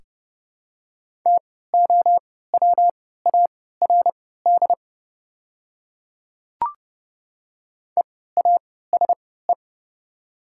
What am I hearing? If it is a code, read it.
Morse code, heard as a keyed tone, about 32 wpm, spaced as for 17 wpm, 700 Hz. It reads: TOWARD EASE
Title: 3x Character Spacing - Top 500 Words - 30wpm